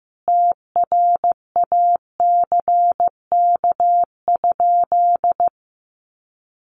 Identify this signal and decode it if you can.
Morse code, heard as a keyed tone, about 15 wpm, 700 Hz. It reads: TRACK?
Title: Sentences from Top 1100 Words 15wpm